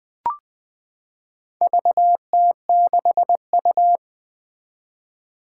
Morse code, heard as a keyed tone, about 20 wpm, 700 Hz. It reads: VT6U
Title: Four Challenging Letter-Number Combinations 20wpm